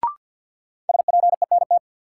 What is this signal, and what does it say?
Morse code, heard as a keyed tone, about 52 wpm, 700 Hz. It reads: SPENT
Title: Sets of 5 Words - New Word Drill - 50wpm